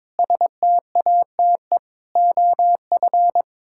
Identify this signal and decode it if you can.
Morse code, heard as a keyed tone, about 22 wpm, 700 Hz. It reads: STATE OF
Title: Pangram Challenge v2 - 22wpm